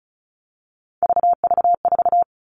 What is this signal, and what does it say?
Morse code, heard as a keyed tone, about 34 wpm, 700 Hz. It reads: VV4
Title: All Cut Numbers - 3-Digit 35wpm